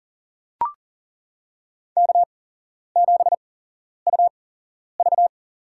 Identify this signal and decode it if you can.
Morse code, heard as a keyed tone, about 39 wpm, spaced as for 11 wpm, 700 Hz. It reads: K7UV
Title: USA Call Signs - 4 Characters - 8x Character Spacing - 40wpm